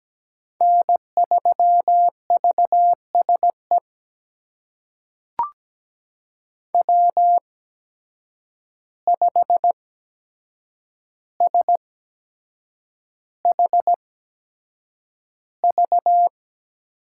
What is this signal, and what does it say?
Morse code, heard as a keyed tone, about 17 wpm, spaced as for 7 wpm, 700 Hz. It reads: N3VSE W5SHV
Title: USA Call Signs - 5 Characters - 8x Character Spacing - 17wpm